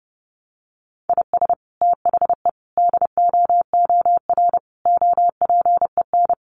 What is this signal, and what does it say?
Morse code, heard as a keyed tone, about 28 wpm, 700 Hz. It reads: IS THE DOOR OPEN
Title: Sentences from Top 1300 Words 30wpm